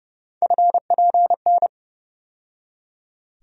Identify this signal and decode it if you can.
Morse code, heard as a keyed tone, about 30 wpm, 700 Hz. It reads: FPN
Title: USA Call Sign Postfixes - 3 Letters 30wpm